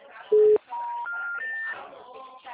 Special information tone, with noise.